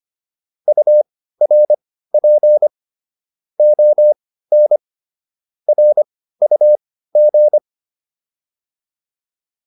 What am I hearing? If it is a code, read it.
Morse code, heard as a keyed tone, about 25 wpm, spaced as for 15 wpm, 600 Hz. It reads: URP ON RUG